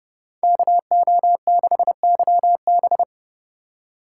Morse code, heard as a keyed tone, about 28 wpm, 700 Hz. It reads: KO6YB